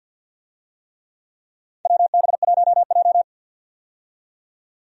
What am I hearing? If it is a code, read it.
Morse code, heard as a keyed tone, about 50 wpm, 700 Hz. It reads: WB1J
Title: USA Call Signs - 4 Characters - 50wpm